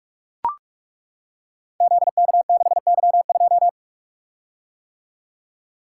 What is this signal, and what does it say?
Morse code, heard as a keyed tone, about 43 wpm, 700 Hz. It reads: ZK6Y2
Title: Code Groups - 5 Letter:Numbers - 45wpm